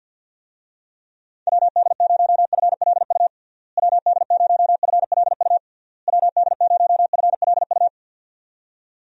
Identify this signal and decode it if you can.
Morse code, heard as a keyed tone, about 48 wpm, 700 Hz. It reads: WD0FLU WD0FLU WD0FLU
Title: USA Call Signs - 6 Characters - Repeated 3x - 50wpm